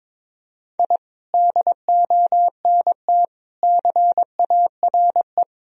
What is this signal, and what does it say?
Morse code, heard as a keyed tone, about 22 wpm, 700 Hz. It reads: I DONT CARE
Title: Sentences from Top 1000 Words - Review Part 4 - 22wpm